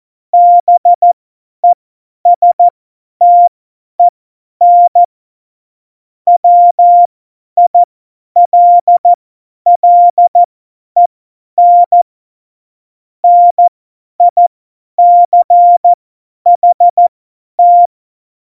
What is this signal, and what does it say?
Morse code, heard as a keyed tone, about 14 wpm, spaced as for 10 wpm, 700 Hz. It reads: BESTEN WILLEN NICHT
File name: Morsekurs / Witze / Witz 20. Oktober 2020, 50 F70